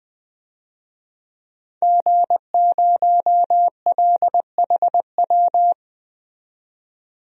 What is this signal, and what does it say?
Morse code, heard as a keyed tone, about 20 wpm, 700 Hz. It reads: G0LHW
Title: DXCC-100 Call Signs - 5 Characters - 20wpm